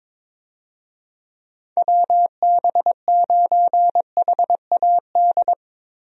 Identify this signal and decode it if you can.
Morse code, heard as a keyed tone, about 22 wpm, 700 Hz. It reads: WB9HAD
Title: USA Call Signs - 6 Characters - Repeated 3x - 22wpm